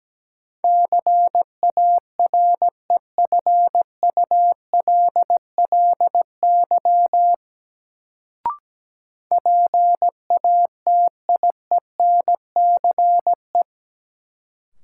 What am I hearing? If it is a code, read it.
Morse code, heard as a keyed tone, about 17 wpm, 700 Hz. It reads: CAREFULLY PATIENCE
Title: Sentences from Top 1300 Words - New Word Drill - 17wpm